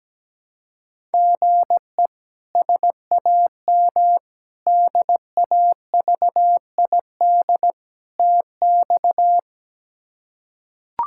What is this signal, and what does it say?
Morse code, heard as a keyed tone, about 17 wpm, 700 Hz. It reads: GE SAM DAVID TX
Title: K1USN SST QSO Practice - 17wpm